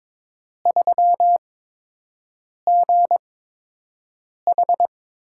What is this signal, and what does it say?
Morse code, heard as a keyed tone, about 22 wpm, spaced as for 6 wpm, 700 Hz. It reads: 3GH
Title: USA Call Signs - 4 Characters - 8x Character Spacing - 22wpm